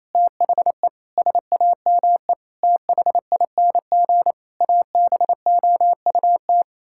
Morse code, heard as a keyed tone, about 28 wpm, 700 Hz. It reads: THE SAME THING ABOUT
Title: Sentences from Top 1000 Words - Review Part 5 - 28wpm